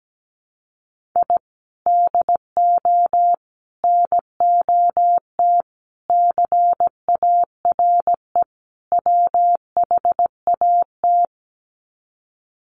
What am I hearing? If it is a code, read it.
Morse code, heard as a keyed tone, about 17 wpm, 700 Hz. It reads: I DO NOT CARE WHAT
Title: Sets of 5 Words 17wpm